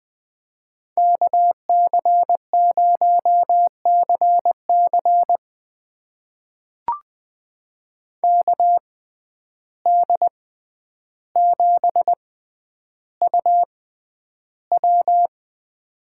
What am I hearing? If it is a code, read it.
Morse code, heard as a keyed tone, about 20 wpm, spaced as for 10 wpm, 700 Hz. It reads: KC0CC KD7UW
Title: USA Call Signs - 5 Characters - 6x Character Spacing - 20wpm